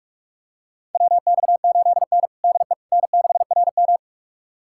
Morse code, heard as a keyed tone, about 48 wpm, 700 Hz. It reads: WX8N DE N6RM
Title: USA Callsign DE Callsign - 4 Characters - 45wpm